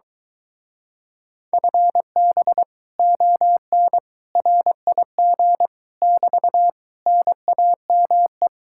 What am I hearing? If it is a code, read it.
Morse code, heard as a keyed tone, about 24 wpm, 700 Hz. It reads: FB ON RIG <BT> NAME